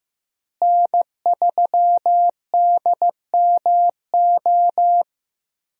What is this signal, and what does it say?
Morse code, heard as a keyed tone, about 15 wpm, 700 Hz. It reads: N3DMO
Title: USA Call Signs - 5 Characters - Repeated 3x - 15wpm